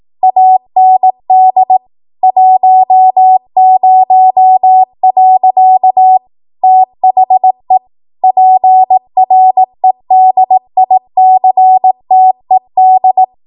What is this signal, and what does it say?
Morse code, heard as a keyed tone, about 18 wpm, 750 Hz. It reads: AND 10. THE PREDICTED
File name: Morse / 18WPM